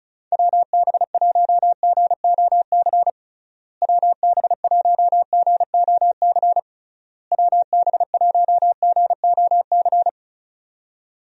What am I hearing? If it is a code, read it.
Morse code, heard as a keyed tone, about 36 wpm, 700 Hz. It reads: WB1GOC WB1GOC WB1GOC
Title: USA Call Signs - 6 Characters - Repeated 3x - 35wpm